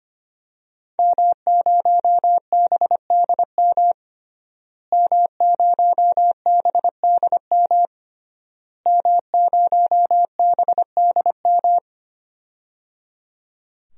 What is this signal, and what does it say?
Morse code, heard as a keyed tone, about 25 wpm, 700 Hz. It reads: M0BDM M0BDM M0BDM
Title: DXCC-100 Call Signs - 5 Characters - Repeated 3x - 25wpm